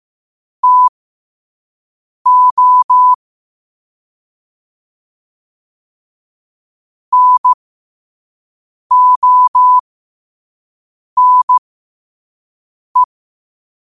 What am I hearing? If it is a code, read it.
Morse code, heard as a keyed tone, about 15 wpm, spaced as for 5 wpm, 1000 Hz. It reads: TO NONE
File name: qotdfiles / 5wpm